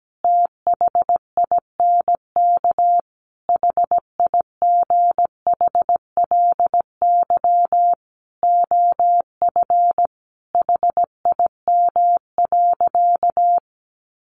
Morse code, heard as a keyed tone, about 17 wpm, 700 Hz. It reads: THINK HIGHLY OF HIM.
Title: Sentences from Top 1100 Words 17wpm